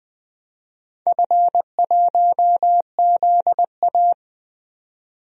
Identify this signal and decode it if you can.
Morse code, heard as a keyed tone, about 20 wpm, 700 Hz. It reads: F1ZA
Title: DXCC-100 Call Signs - 4 Characters - 20wpm